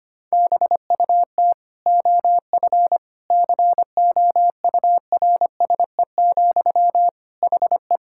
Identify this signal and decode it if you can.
Morse code, heard as a keyed tone, about 25 wpm, 700 Hz. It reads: BUT OF COURSE, HE
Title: Sentences from Top 1600 Words 25wpm